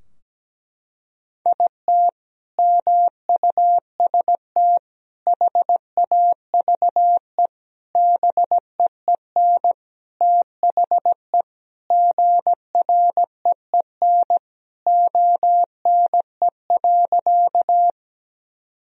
Morse code, heard as a keyed tone, about 17 wpm, 700 Hz. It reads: IT MUST HAVE BEEN THE GREEN ONE.
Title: Sentences from Top 1000 Words - Review Part 2 - 17wpm